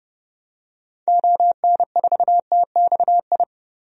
Morse code, heard as a keyed tone, about 30 wpm, 700 Hz. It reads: ON4TXI
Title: DXCC-100 Call Signs - 6 Characters - 30wpm